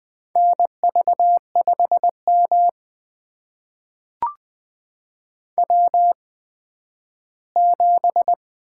Morse code, heard as a keyed tone, about 20 wpm, spaced as for 12 wpm, 700 Hz. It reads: NV5M W7
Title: USA Call Signs - 4 Characters - 8x Character Spacing - 20wpm